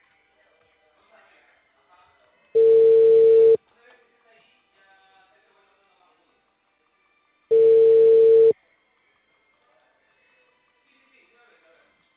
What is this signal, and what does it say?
Ringback tone (China, 450 Hz), with speech.